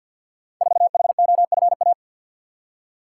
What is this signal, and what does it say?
Morse code, heard as a keyed tone, about 50 wpm, 700 Hz. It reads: 4HOFA
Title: Code Groups - 5 Letter:Numbers - 50wpm